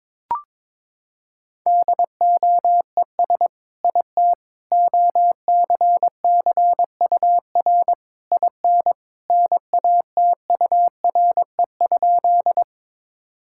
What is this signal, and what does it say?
Morse code, heard as a keyed tone, about 22 wpm, 700 Hz. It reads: DOES IT OCCUR IN NATURE?
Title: Sentences from Top 1000 Words - Review Part 6 - 22wpm